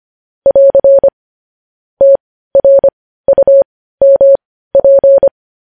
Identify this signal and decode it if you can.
Morse code, heard as a keyed tone, about 25 wpm, spaced as for 15 wpm, 550 Hz. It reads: <AR> TRUMP
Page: News Headlines in Morse Code at 15 WPM